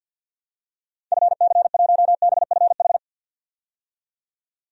Morse code, heard as a keyed tone, about 48 wpm, 700 Hz. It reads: FC1BFH
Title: DXCC-100 Call Signs - 6 Characters - 50wpm